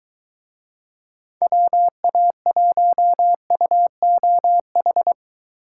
Morse code, heard as a keyed tone, about 21 wpm, 700 Hz. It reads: WA1UOH